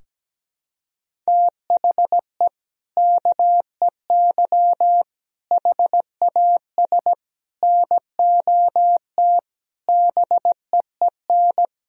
Morse code, heard as a keyed tone, about 17 wpm, 700 Hz. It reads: THE KEY HAS NOT BEEN
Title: Sentences from Top 1000 Words - Review Part 4 - 17wpm